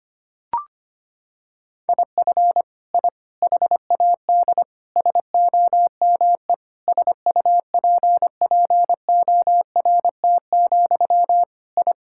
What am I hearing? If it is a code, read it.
Morse code, heard as a keyed tone, about 25 wpm, 700 Hz. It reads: IF I HAD SOME SUPPORT, I
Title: Sentences from Top 1000 Words - Review Part 5 - 25wpm